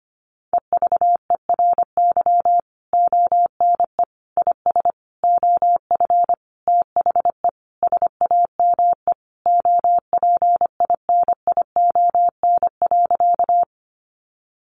Morse code, heard as a keyed tone, about 25 wpm, 700 Hz. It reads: EVERY ONE IS OF THE SAME OPINION.
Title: Sentences from Top 1200 Words 25wpm